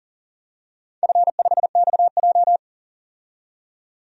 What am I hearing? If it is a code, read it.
Morse code, heard as a keyed tone, about 40 wpm, 700 Hz. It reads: F5XJ
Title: DXCC-100 Call Signs - 4 Characters - 40wpm